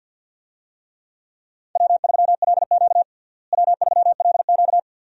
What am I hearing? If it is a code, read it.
Morse code, heard as a keyed tone, about 46 wpm, 700 Hz. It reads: W3LQ W3LQ
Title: USA Call Signs - 4 Characters - Repeated 2x - 50wpm